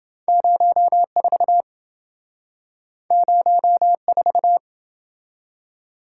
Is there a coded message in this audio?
Yes: Morse code from 0.3 to 4.6 s, heard as a keyed tone, about 27 wpm, 700 Hz. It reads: 04 04